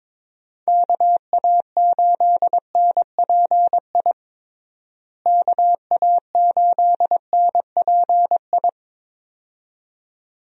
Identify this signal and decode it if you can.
Morse code, heard as a keyed tone, about 22 wpm, 700 Hz. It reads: KA8NPI KA8NPI
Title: USA Call Signs - 6 Characters - Repeated 2x - 22wpm